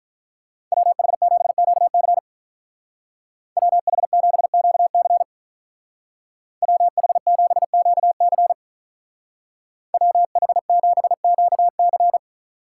Morse code, heard as a keyed tone, about 47 wpm, 700 Hz. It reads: WH7QC WH7QC WH7QC WH7QC